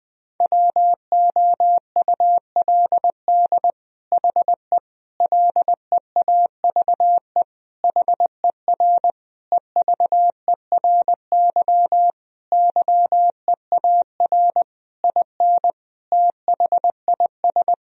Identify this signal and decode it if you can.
Morse code, heard as a keyed tone, about 20 wpm, 700 Hz. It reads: WOULD HE LEAVE HER EVERY YEAR IN THIS
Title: Sentences from Top 1000 Words - Review Part 1 - 20wpm